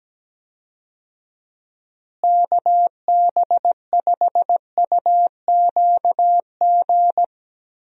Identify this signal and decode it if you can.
Morse code, heard as a keyed tone, about 17 wpm, 700 Hz. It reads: KB5UQG
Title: USA Call Signs - 6 Characters - Repeated 3x - 17wpm